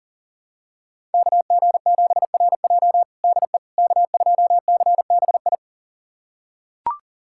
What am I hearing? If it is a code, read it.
Morse code, heard as a keyed tone, about 40 wpm, 700 Hz. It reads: KG7RJ DE K2CBI
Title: USA Callsign DE Callsign - 5 Characters - 40wpm